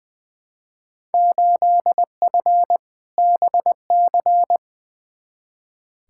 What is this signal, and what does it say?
Morse code, heard as a keyed tone, about 20 wpm, 700 Hz. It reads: 8F BC
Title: Field Day Exchanges Encore - 20wpm